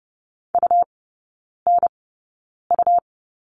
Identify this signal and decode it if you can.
Morse code, heard as a keyed tone, about 30 wpm, spaced as for 9 wpm, 700 Hz. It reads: UNU
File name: Lesson 17e - Letter u - Three Characters 30wpm